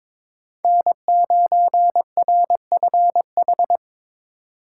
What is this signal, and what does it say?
Morse code, heard as a keyed tone, about 25 wpm, 700 Hz. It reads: N9RFH
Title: Speed Racing - USA Call Signs 22wpm